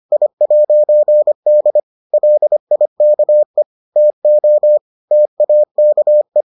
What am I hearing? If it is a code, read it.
Morse code, heard as a keyed tone, about 25 wpm, 600 Hz. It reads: I'D LIKE TO TAKE